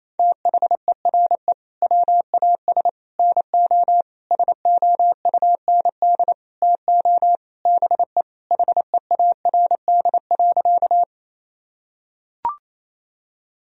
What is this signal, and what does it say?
Morse code, heard as a keyed tone, about 28 wpm, 700 Hz. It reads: THERE WAS NO SOUND TO BE HEARD.
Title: Sentences from Top 1000 Words - Review Part 1 - 28wpm